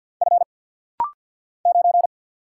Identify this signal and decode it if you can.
Morse code, heard as a keyed tone, about 50 wpm, 700 Hz. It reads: F 9
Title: Single Letter-Number Alt - Rapid-Fire 50wpm